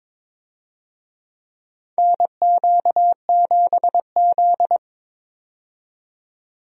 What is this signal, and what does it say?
Morse code, heard as a keyed tone, about 22 wpm, 700 Hz. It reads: NQ7Z
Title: USA Call Signs - 4 Characters - 22wpm